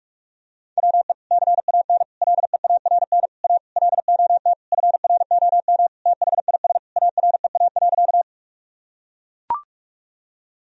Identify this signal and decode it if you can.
Morse code, heard as a keyed tone, about 42 wpm, 700 Hz. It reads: WE CAN LEARN A LOT FROM THIS AREA.
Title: Sentences from Top 1000 Words - Review Part 2 - 45wpm